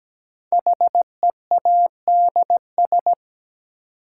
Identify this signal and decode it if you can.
Morse code, heard as a keyed tone, about 17 wpm, 700 Hz. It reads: HEADS
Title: Sentences from Top 1000 Words - Review Part 3 - New Word Drill - 17wpm